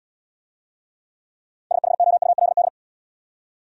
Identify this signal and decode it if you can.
Morse code, heard as a keyed tone, about 71 wpm, 700 Hz. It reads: SHOULD